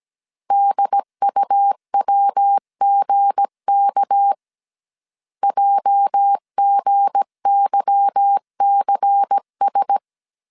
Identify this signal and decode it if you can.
Morse code, heard as a keyed tone, about 17 wpm, 800 Hz. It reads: DUWGK JGYCS
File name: CW / DET / 1 9 1500C S A 16wpm 2828